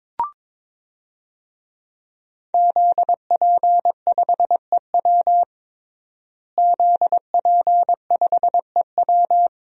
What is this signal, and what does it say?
Morse code, heard as a keyed tone, about 22 wpm, 700 Hz. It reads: ZP5EW ZP5EW
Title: DXCC-100 Call Signs - 5 Characters - Repeated 2x - 22wpm